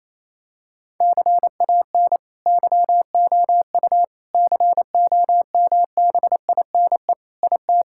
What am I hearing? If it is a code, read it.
Morse code, heard as a keyed tone, about 28 wpm, 700 Hz. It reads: CAN YOU COMBINE IT
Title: Sentences from Top 1600 Words 28wpm